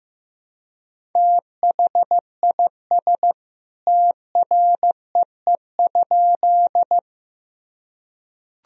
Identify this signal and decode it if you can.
Morse code, heard as a keyed tone, about 15 wpm, 700 Hz. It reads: THIS TREE?
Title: Sentences from Top 1000 Words - Review Part 1 - 15wpm